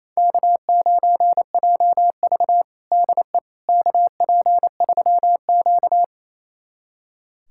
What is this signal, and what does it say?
Morse code, heard as a keyed tone, about 28 wpm, 700 Hz. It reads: K9JV DE KP3Q
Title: USA Callsign DE Callsign - 4 Characters - 28wpm